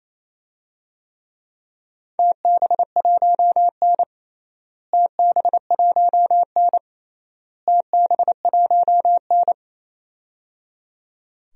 Morse code, heard as a keyed tone, about 28 wpm, 700 Hz. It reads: TB1N TB1N TB1N
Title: DXCC-100 Call Signs - 4 Characters - Repeated 3x - 28wpm